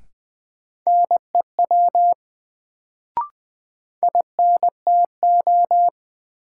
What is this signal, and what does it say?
Morse code, heard as a keyed tone, about 20 wpm, 700 Hz. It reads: NEW INTO
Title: Sets of 2 Words - New Word Drill - 20wpm